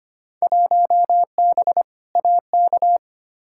Morse code, heard as a keyed tone, about 25 wpm, 700 Hz. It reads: 1B AK
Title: Field Day Exchanges Encore - 25wpm